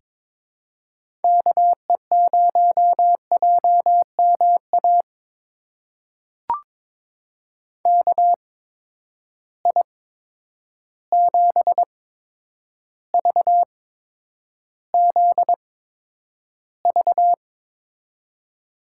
Morse code, heard as a keyed tone, about 22 wpm, spaced as for 9 wpm, 700 Hz. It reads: KE0JMA KI7VZV
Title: USA Call Signs - 6 Characters - 8x Character Spacing - 22wpm